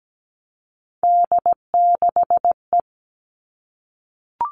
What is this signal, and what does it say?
Morse code, heard as a keyed tone, about 17 wpm, 700 Hz. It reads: D6E
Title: Three Challenging Letter-Number Combinations 17wpm